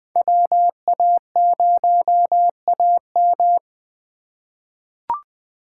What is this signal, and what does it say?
Morse code, heard as a keyed tone, about 21 wpm, 700 Hz. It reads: WA0AM
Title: USA Call Signs - 5 Characters - 4x Character Spacing - 20wpm